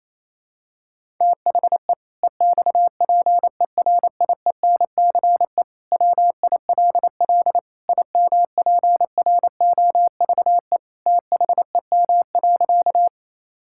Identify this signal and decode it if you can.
Morse code, heard as a keyed tone, about 28 wpm, 700 Hz. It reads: THE EXPERIENCE WILL IMPROVE THEM.
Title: Sentences from Top 1000 Words - Review Part 5 - 28wpm